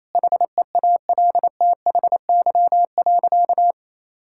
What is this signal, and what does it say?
Morse code, heard as a keyed tone, about 28 wpm, 700 Hz. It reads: HEALTHY.
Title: Sentences from Top 1000 Words - Review Part 5 - 28wpm